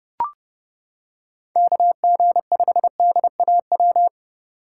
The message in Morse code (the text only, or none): KG5DAW